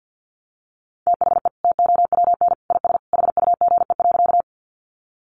Morse code, heard as a keyed tone, about 50 wpm, 700 Hz. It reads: THE TOWN IS HUGE.
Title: Sentences from Top 1100 Words 50wpm